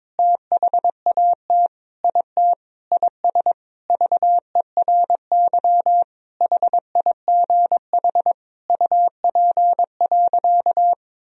Morse code, heard as a keyed tone, about 22 wpm, 700 Hz. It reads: THAT IT IS VERY HIGH UP.
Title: Sentences from Top 1000 Words - Review Part 1 - 22wpm